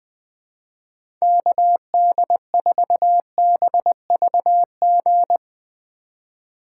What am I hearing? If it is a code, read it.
Morse code, heard as a keyed tone, about 20 wpm, 700 Hz. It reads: KD4BVG
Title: 2023 POTA Activator Call Signs - 20wpm